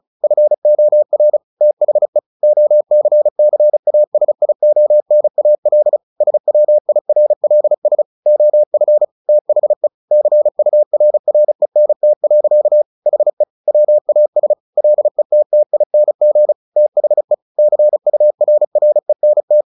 Morse code, heard as a keyed tone, about 34 wpm, 600 Hz. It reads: FOR THE OCCASIONAL SWIRLS OF THE CURRENT. HE WAS LETTING THE CURRENT
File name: the old man and the sea / the old man and the sea 1 35wpm